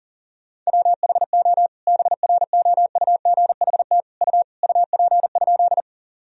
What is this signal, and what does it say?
Morse code, heard as a keyed tone, about 40 wpm, 700 Hz. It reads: WHO BROUGHT U UP?
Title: Sentences from Top 500 Words and Abbreviations 40wpm